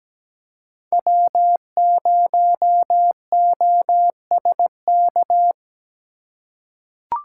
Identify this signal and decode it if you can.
Morse code, heard as a keyed tone, about 17 wpm, 700 Hz. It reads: W0OSK
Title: USA Call Signs - 5 Characters - 2x Character Spacing - 17wpm